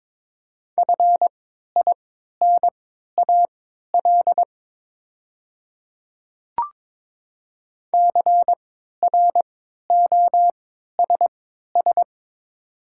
Morse code, heard as a keyed tone, about 22 wpm, spaced as for 13 wpm, 700 Hz. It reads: FINAL CROSS
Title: 3x Character Spacing - Top 500 Words - 22wpm